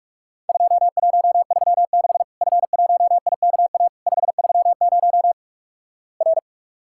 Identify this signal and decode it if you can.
Morse code, heard as a keyed tone, about 47 wpm, 700 Hz. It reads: 2136 F1IKA 530 R
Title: Worked All Europe DX Contest - QTC Exchange - 45wpm